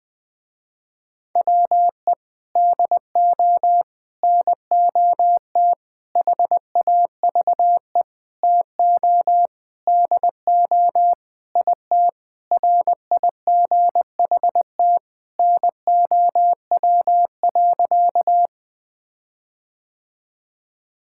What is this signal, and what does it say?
Morse code, heard as a keyed tone, about 21 wpm, 700 Hz. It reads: WE DO NOT HAVE TO DO IT RIGHT NOW.
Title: Sentences with 5-Word Phrases - Spoken Beginning - 20wpm